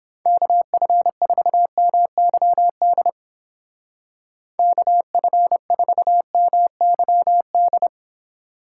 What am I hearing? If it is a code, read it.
Morse code, heard as a keyed tone, about 29 wpm, 700 Hz. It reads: KF4MYD KF4MYD